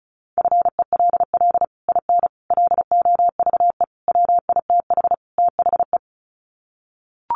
Morse code, heard as a keyed tone, about 34 wpm, 700 Hz. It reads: FELL IN LOVE WITH THE
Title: Sets of 5 Words Encore2 35wpm